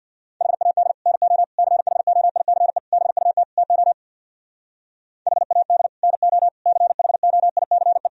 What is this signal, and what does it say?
Morse code, heard as a keyed tone, about 64 wpm, 690 Hz. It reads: HAD NO CHOICE BUT TO HAD NO CHOICE